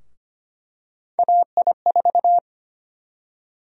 Morse code, heard as a keyed tone, about 25 wpm, 700 Hz. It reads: AI4
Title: USA Call Sign Prefixes - 25wpm